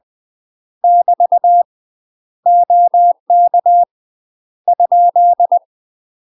Morse code, heard as a keyed tone, about 20 wpm, 700 Hz. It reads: <BT> OK ?
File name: UK MASTER QSOs TRAINING / 20WPM / UK MASTER QSOs 0014